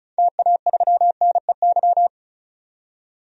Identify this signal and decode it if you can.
Morse code, heard as a keyed tone, about 36 wpm, 700 Hz. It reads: TA3NEY
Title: DXCC-100 Call Signs - 6 Characters - 35wpm